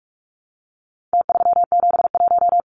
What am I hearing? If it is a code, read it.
Morse code, heard as a keyed tone, about 44 wpm, 700 Hz. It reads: T371
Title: Numbers - Contest Style - 4-Digit 45wpm